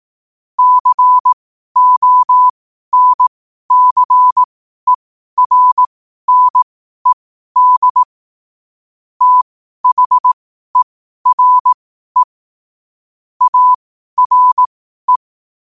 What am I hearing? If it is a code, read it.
Morse code, heard as a keyed tone, about 18 wpm, spaced as for 12 wpm, 1000 Hz. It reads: CONCERNED THERE ARE
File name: qotdfiles / 12wpm